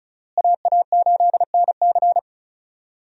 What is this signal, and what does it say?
Morse code, heard as a keyed tone, about 34 wpm, 700 Hz. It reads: AA8NC
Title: USA Call Signs - 5 Characters - Repeated 3x - 35wpm